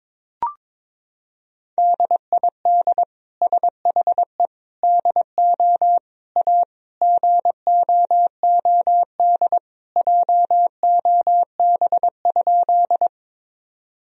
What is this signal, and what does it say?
Morse code, heard as a keyed tone, about 22 wpm, 700 Hz. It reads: DID SHE DO A GOOD JOB?
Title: Sentences from Top 1000 Words - Review Part 3 - 22wpm